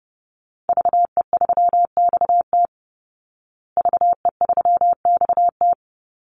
Morse code, heard as a keyed tone, about 30 wpm, 700 Hz. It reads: VE3XT VE3XT
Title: Call Signs Repeated 2x - 30wpm